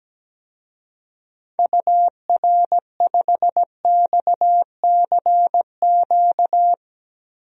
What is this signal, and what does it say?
Morse code, heard as a keyed tone, about 17 wpm, 700 Hz. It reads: UR5XCQ
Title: DXCC-100 Call Signs - 6 Characters - 17wpm